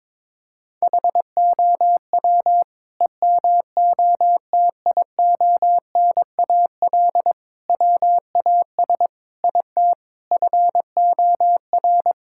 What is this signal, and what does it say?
Morse code, heard as a keyed tone, about 22 wpm, 700 Hz. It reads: HOW EMOTIONAL WAS IT FOR
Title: Sentences from Top 1000 Words - Review Part 5 - 22wpm